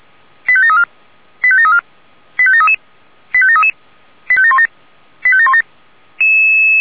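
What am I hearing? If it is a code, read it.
Five-tone selective call (ZVEI), heard as quick runs of five tones, tones of about 70 ms. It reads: 85734 85734 85733 85733 85717 85717
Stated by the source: ZVEI five-tone selective call 85942 received by SDR.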